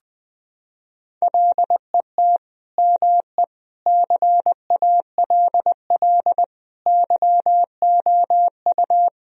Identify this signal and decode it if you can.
Morse code, heard as a keyed tone, about 21 wpm, 700 Hz. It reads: LET ME CALL YOU
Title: Sentences from Top 1000 Words - Review Part 2 - 20wpm